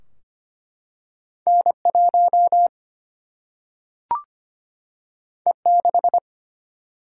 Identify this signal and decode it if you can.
Morse code, heard as a keyed tone, about 25 wpm, 700 Hz. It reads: N1 E6
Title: Single-Letter Single-Number 25wpm